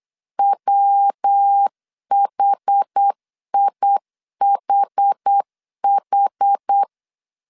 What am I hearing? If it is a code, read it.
Morse code, heard as a keyed tone, about 8 wpm, 800 Hz. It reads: WHIHH